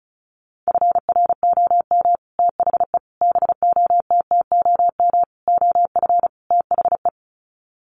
Morse code, heard as a keyed tone, about 34 wpm, 700 Hz. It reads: FROM THE BOTTOM OF THE
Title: Sets of 5 Words Encore2 35wpm